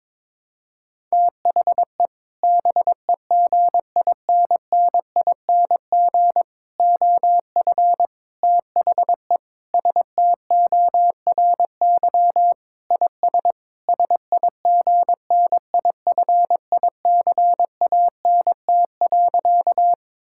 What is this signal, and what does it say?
Morse code, heard as a keyed tone, about 22 wpm, 700 Hz. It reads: THE BEGINNING OF THE STORY IS SIGNIFICANT.
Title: Sentences from Top 1000 Words - Review Part 6 - 22wpm